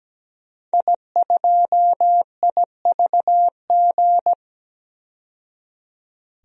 Morse code, heard as a keyed tone, about 17 wpm, 700 Hz. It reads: I2IVG
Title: DXCC-100 Call Signs - 5 Characters - 17wpm